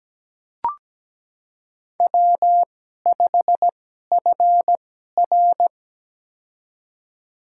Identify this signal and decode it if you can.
Morse code, heard as a keyed tone, about 17 wpm, spaced as for 12 wpm, 700 Hz. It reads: W5FR
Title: USA Call Signs - 4 Characters - 2x Character Spacing - 17wpm